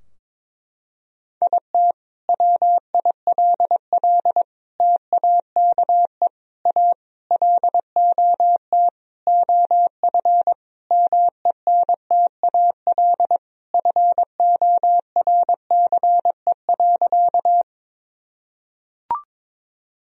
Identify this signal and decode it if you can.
Morse code, heard as a keyed tone, about 22 wpm, 700 Hz. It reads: IT WILL TAKE A LOT OF MENTAL FORCE.
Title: Sentences from Top 1000 Words - Review Part 4 - 22wpm